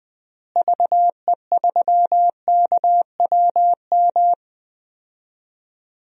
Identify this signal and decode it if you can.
Morse code, heard as a keyed tone, about 20 wpm, 700 Hz. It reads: VE3KWM
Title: DXCC-100 Call Signs - 6 Characters - Repeated 3x - 20wpm